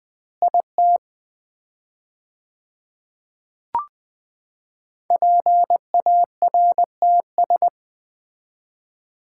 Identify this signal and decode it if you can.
Morse code, heard as a keyed tone, about 21 wpm, 690 Hz. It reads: IT PARTS
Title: 4x Word-Spacing - Sets of 3 Words - 20wpm